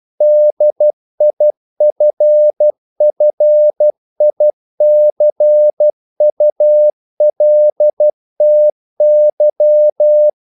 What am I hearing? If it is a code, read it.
Morse code, heard as a keyed tone, about 12 wpm, 600 Hz. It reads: DIFFICULTY
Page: English quotations in CW (telegraphy - morse)